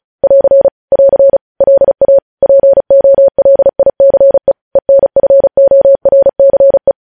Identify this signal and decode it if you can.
Morse code, heard as a keyed tone, about 36 wpm, 550 Hz. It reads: <AR> <AR> LA POLICE ENFORCE
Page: news headlines in morse code at 35 wpm